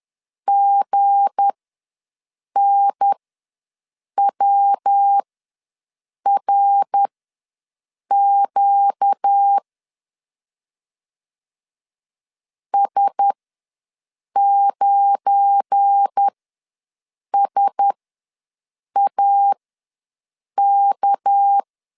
Morse code, heard as a keyed tone, about 10 wpm, spaced as for 6 wpm, 800 Hz. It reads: GNWRQ S9SAK
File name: LEZIONI / 7 GF9Z / 7 GF9 ws 10wpm 4034